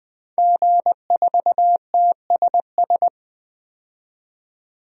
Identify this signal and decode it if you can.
Morse code, heard as a keyed tone, about 20 wpm, 700 Hz. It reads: G4TSS
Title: DXCC-100 Call Signs - 5 Characters - 20wpm